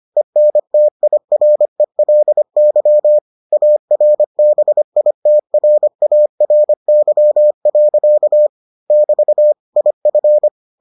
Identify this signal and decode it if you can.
Morse code, heard as a keyed tone, about 25 wpm, 600 Hz. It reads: ENTIRELY ARBITRARY. <BT> IF